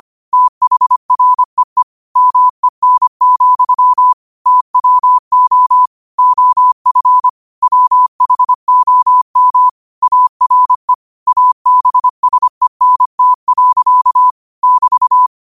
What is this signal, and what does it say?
Morse code, heard as a keyed tone, about 25 wpm, 1000 Hz. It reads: THREE MEN, TWO OF WHOM ARE ABSENT. <BT>
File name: qotdfiles / 25wpm